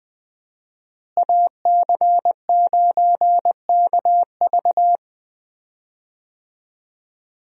Morse code, heard as a keyed tone, about 20 wpm, 700 Hz. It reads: AC9KV
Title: USA Call Signs - 5 Characters - Repeated 2x - 20wpm